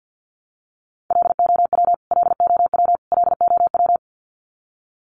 Morse code, heard as a keyed tone, about 50 wpm, 700 Hz. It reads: LOW LOW LOW
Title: Top 200 Words Repeated 50wpm